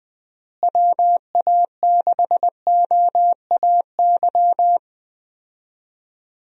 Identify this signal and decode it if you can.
Morse code, heard as a keyed tone, about 20 wpm, 700 Hz. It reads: WA6OAY
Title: USA Call Signs - 6 Characters - Repeated 2x - 20wpm